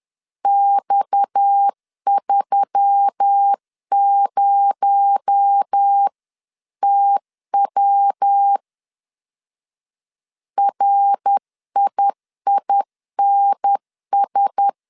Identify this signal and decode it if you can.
Morse code, heard as a keyed tone, about 10 wpm, 800 Hz. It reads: X30TW RIINS
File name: CW / DET / 1 6 1500C S C 10wpm 4252